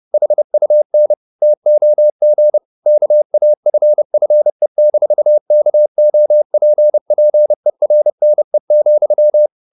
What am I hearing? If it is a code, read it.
Morse code, heard as a keyed tone, about 30 wpm, 600 Hz. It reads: HUN TOG KAFFE-KOPPERNE,
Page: Ferie 30 wpm